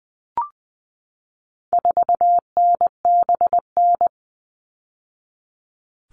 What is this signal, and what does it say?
Morse code, heard as a keyed tone, about 20 wpm, 700 Hz. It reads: 4NBN
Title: All Cut Numbers - 4-Digit 20wpm